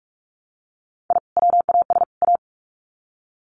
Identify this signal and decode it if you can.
Morse code, heard as a keyed tone, about 44 wpm, 700 Hz. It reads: I WAS A
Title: Sets of 3 Words 45wpm